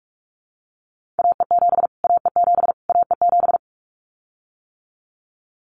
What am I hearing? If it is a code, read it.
Morse code, heard as a keyed tone, about 44 wpm, 700 Hz. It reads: AE7 AE7 AE7
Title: Callsign Prefixes for Top 25 DXCC - Repeated 3x - 45wpm